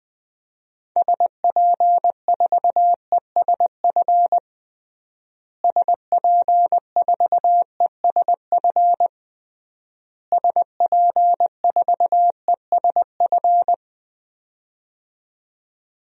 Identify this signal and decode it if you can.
Morse code, heard as a keyed tone, about 20 wpm, 700 Hz. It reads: SP4ESF SP4ESF SP4ESF
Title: DXCC-100 Call Signs - 6 Characters - Repeated 3x - 20wpm